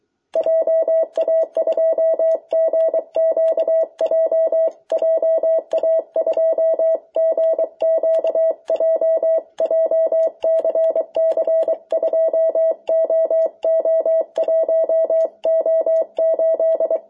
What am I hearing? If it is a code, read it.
Morse code, heard as a keyed tone, about 24 wpm, 650 Hz. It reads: JA2GQJ JA2GQJ JCC 2OO1O8
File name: jh8cbh / 2021cw